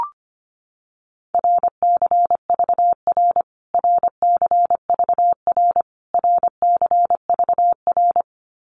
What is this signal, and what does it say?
Morse code, heard as a keyed tone, about 25 wpm, 700 Hz. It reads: RCVR RCVR RCVR
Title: Abbreviations Repeated 3x - 25wpm